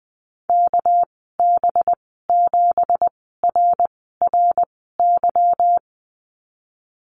Morse code, heard as a keyed tone, about 20 wpm, spaced as for 14 wpm, 700 Hz. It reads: KB7RRY
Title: 200 Lesson 32g - Letter k - Call Signs 20wpm